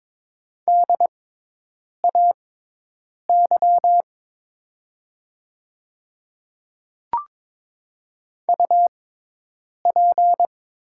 Morse code, heard as a keyed tone, about 22 wpm, spaced as for 8 wpm, 700 Hz. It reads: DAY UP
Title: 6x Character Spacing - Top 100 Words - 22wpm